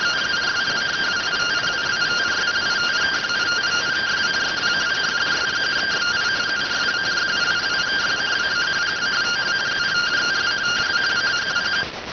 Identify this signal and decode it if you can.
Two-tone teleprinter signal, about 170 Hz shift.